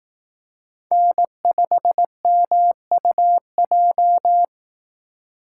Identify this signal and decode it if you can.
Morse code, heard as a keyed tone, about 18 wpm, 700 Hz. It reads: N5MUJ
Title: Speed Racing - USA 1x3 Call Signs 15wpm